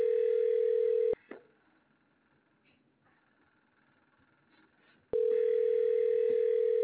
Ringback tone (North America), 440 + 480 Hz.